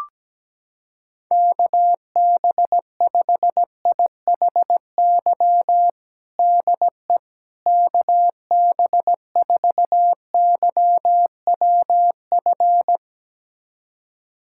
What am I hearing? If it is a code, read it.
Morse code, heard as a keyed tone, about 17 wpm, 700 Hz. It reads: KB5IHY DE KB4YWF
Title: USA Callsign DE Callsign - 6 Characters - 17wpm